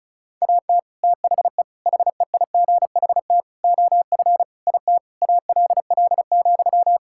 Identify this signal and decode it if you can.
Morse code, heard as a keyed tone, about 36 wpm, 700 Hz. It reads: AT THE HEIGHT OF IT ALL,
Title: Sentences from Top 1600 Words 35wpm